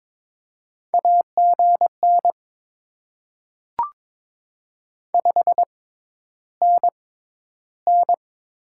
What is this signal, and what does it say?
Morse code, heard as a keyed tone, about 22 wpm, spaced as for 11 wpm, 700 Hz. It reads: AGN 5NN
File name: Top 20 QSO Elements - 6x Character Spacing - 22wpm